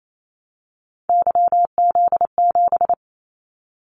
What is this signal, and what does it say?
Morse code, heard as a keyed tone, about 28 wpm, 700 Hz. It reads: YZ7
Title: Callsign Prefixes for Top 25 DXCC - Repeated 3x - 28wpm